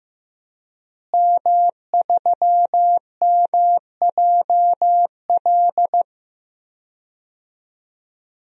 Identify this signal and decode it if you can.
Morse code, heard as a keyed tone, about 15 wpm, 700 Hz. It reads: M3MJL
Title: DXCC-100 Call Signs - 5 Characters - Repeated 2x - 15wpm